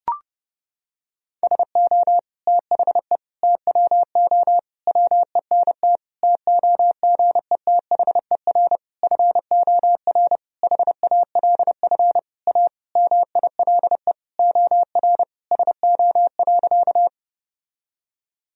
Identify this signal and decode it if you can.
Morse code, heard as a keyed tone, about 30 wpm, 700 Hz. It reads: SO THE TWO WENT TOGETHER FOR HALF A MILE OR SO.
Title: Sentences from Top 1000 Words - Review Part 1 - 30wpm